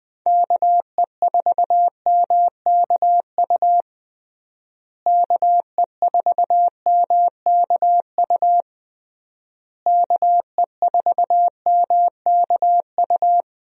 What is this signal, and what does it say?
Morse code, heard as a keyed tone, about 21 wpm, 700 Hz. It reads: KE4MKU KE4MKU KE4MKU
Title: USA Call Signs - 6 Characters - Repeated 3x - 20wpm